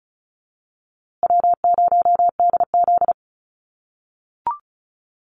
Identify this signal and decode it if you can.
Morse code, heard as a keyed tone, about 34 wpm, 700 Hz. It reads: W0DZ
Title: USA Call Signs Repeated 2x 35wpm